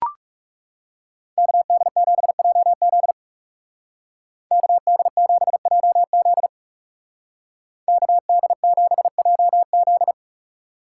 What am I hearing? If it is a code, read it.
Morse code, heard as a keyed tone, about 36 wpm, 700 Hz. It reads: KD7JZ KD7JZ KD7JZ
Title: Speed Racing - USA Call Signs 30wpm